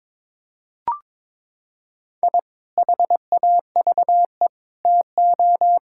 Morse code, heard as a keyed tone, about 22 wpm, 700 Hz. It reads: I HAVE TO
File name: practice / 22wpm / Sentences with 3-Word Phrases - 22wpm